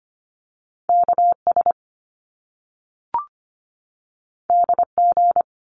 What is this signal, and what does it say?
Morse code, heard as a keyed tone, about 25 wpm, 700 Hz. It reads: K S D G
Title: Top Two Letter Combinations in Words 25wpm